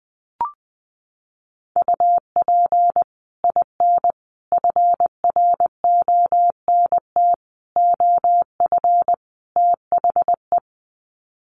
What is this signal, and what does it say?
Morse code, heard as a keyed tone, about 20 wpm, 700 Hz. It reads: UP IN FRONT OF THE
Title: Sets of 5 Words Encore2 20wpm